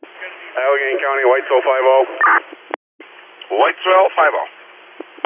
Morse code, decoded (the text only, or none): none